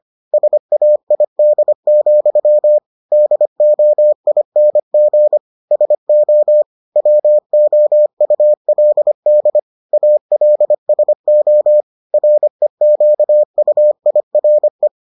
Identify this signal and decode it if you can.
Morse code, heard as a keyed tone, about 25 wpm, 600 Hz. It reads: SAID, DOING SO WOULD ALSO REQUIRE